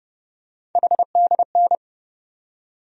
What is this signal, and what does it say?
Morse code, heard as a keyed tone, about 32 wpm, 700 Hz. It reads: HDN
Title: USA Call Sign Postfixes - 3 Letters 30wpm